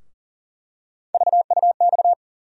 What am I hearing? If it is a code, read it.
Morse code, heard as a keyed tone, about 40 wpm, 700 Hz. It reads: VUX
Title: USA Call Sign Postfixes - 3 Letters 40wpm